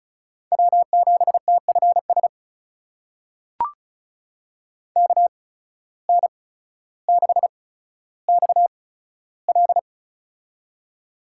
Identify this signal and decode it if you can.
Morse code, heard as a keyed tone, about 34 wpm, spaced as for 15 wpm, 700 Hz. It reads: W7TFS KN6XL
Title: USA Call Signs - 5 Characters - 8x Character Spacing - 35wpm